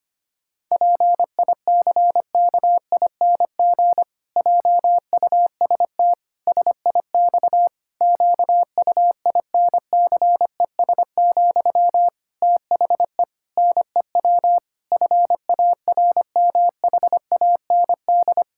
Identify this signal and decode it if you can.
Morse code, heard as a keyed tone, about 25 wpm, 700 Hz. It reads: PICKING JUST SIX QUINCES, THE NEW FARMHAND
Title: Pangram Challenge v2 - 25wpm